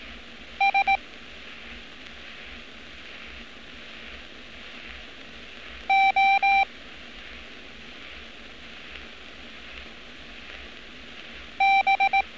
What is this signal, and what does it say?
Morse code, heard as a keyed tone, about 18 wpm, 780 Hz. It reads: S O B
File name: training / NAVPERS 11200-11211 / NAVPERS 11206 RA RB